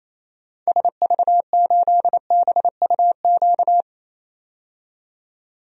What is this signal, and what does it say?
Morse code, heard as a keyed tone, about 28 wpm, 700 Hz. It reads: SV8BUQ
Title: DXCC-100 Call Signs - 6 Characters - 28wpm